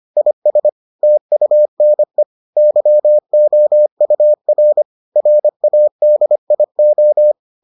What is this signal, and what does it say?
Morse code, heard as a keyed tone, about 25 wpm, 600 Hz. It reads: IS TUNE YOUR RADIO